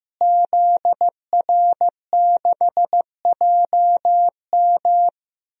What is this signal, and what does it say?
Morse code, heard as a keyed tone, about 15 wpm, 700 Hz. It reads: ZR6JM
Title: DXCC-100 Call Signs - 5 Characters - Repeated 2x - 15wpm